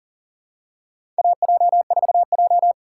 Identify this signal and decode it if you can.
Morse code, heard as a keyed tone, about 40 wpm, 700 Hz. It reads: AJ4J
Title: USA Call Signs - 4 Characters - 6x Character Spacing - 40wpm